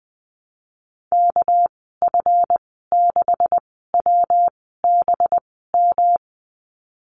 Morse code, heard as a keyed tone, about 20 wpm, spaced as for 15 wpm, 700 Hz. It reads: KF6WBM
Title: 207 Lesson 33g - Letter j - Call Signs 20wpm